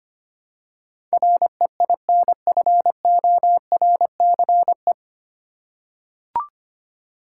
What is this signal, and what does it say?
Morse code, heard as a keyed tone, about 25 wpm, 700 Hz. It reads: REINFORCE
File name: practice / 25wpm / 9 Letter Words - First 3 Letters Given - 25wpm